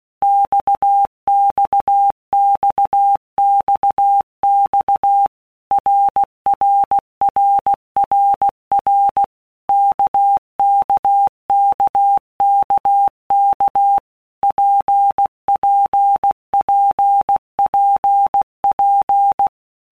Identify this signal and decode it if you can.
Morse code, heard as a keyed tone, about 16 wpm, 800 Hz. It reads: XXXXX RRRRR KKKKK PPPPP
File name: Basic Letter Groups By Blocks / Groups@15 / RKPX@15